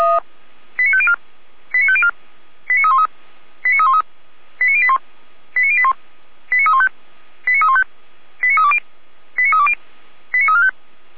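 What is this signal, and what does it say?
Five-tone selective call (ZVEI), heard as quick runs of five tones, tones of about 70 ms. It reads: 89504 89504 89313 89313 89081 89081 89315 89315 89320 89320 89345
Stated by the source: ZVEI five-tone selective call 89305 received via SDR, from the Feuerwehr Orlamünde server.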